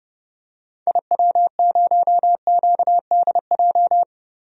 Morse code, heard as a keyed tone, about 31 wpm, 700 Hz. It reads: IW0QDJ
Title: DXCC-100 Call Signs - 6 Characters - 30wpm